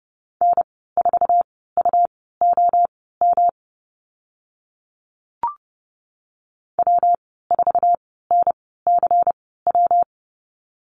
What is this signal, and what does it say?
Morse code, heard as a keyed tone, about 32 wpm, spaced as for 17 wpm, 700 Hz. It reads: N4UOM W4NCW
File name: Lesson 19g - Letter w - Call Signs 30wpm